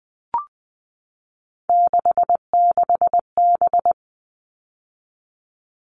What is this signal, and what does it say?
Morse code, heard as a keyed tone, about 20 wpm, 700 Hz. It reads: 66B
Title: All Cut Numbers - 3-Digit 20wpm